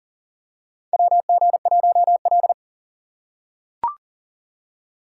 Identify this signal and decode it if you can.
Morse code, heard as a keyed tone, about 40 wpm, 700 Hz. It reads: WG1L
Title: USA Call Signs - 4 Characters - 2x Character Spacing - 40wpm